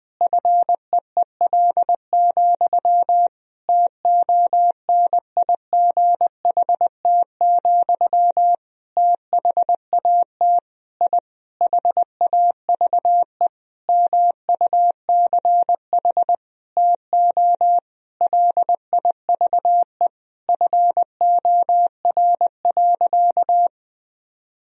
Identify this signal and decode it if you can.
Morse code, heard as a keyed tone, about 21 wpm, 700 Hz. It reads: FEEL, TONIGHT, THAT I HAVE MUCH TO LIVE FOR.
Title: Sentences from Top 1000 Words - Review Part 3 - 20wpm